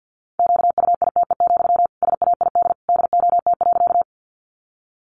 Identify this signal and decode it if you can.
Morse code, heard as a keyed tone, about 50 wpm, 700 Hz. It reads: QUITE, SAID DOT.
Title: Sentences from Top 1500 Words 50wpm